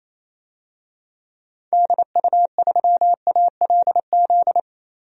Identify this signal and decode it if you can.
Morse code, heard as a keyed tone, about 28 wpm, 700 Hz. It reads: DU3ALZ
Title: DXCC-100 Call Signs - 6 Characters - 28wpm